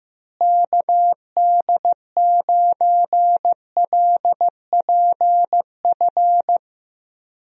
Morse code, heard as a keyed tone, about 15 wpm, 700 Hz. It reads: KD9LPF
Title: USA Call Signs - 6 Characters - Repeated 2x - 15wpm